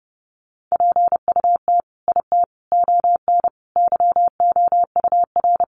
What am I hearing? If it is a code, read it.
Morse code, heard as a keyed tone, about 28 wpm, 700 Hz. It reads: PUT IT ON YOUR
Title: Sentences from Top 1400 Words 30wpm